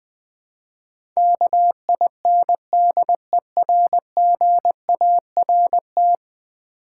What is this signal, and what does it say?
Morse code, heard as a keyed tone, about 20 wpm, 700 Hz. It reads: KINDERGART
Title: Build A Word - 5 Letters At A Time - 12-Letter Words - 20wpm